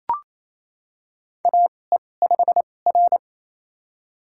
Morse code, heard as a keyed tone, about 27 wpm, spaced as for 20 wpm, 700 Hz. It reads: AE5R
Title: USA Call Signs - 4 Characters - 2x Character Spacing - 28wpm